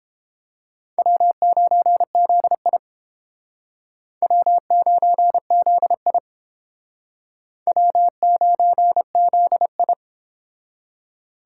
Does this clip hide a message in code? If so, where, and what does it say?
Yes: Morse code from 1.0 to 9.9 s, heard as a keyed tone, about 31 wpm, 700 Hz. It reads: W9ZI W9ZI W9ZI